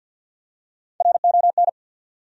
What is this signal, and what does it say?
Morse code, heard as a keyed tone, about 53 wpm, 700 Hz. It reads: RON
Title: USA Common Names - Max 3 Letters - 50wpm